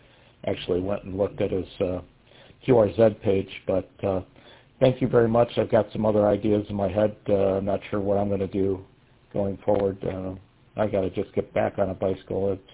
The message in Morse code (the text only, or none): none